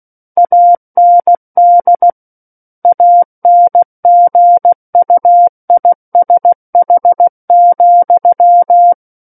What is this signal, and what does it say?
Morse code, heard as a keyed tone, about 16 wpm, 700 Hz. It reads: AND ANGUISH,
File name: Book One / 07-Chapter 7 - 16 WPM